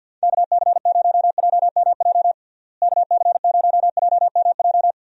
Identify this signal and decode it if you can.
Morse code, heard as a keyed tone, about 51 wpm, 700 Hz. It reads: KC0JMJ KC0JMJ
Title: USA Call Signs - 6 Characters - Repeated 3x - 50wpm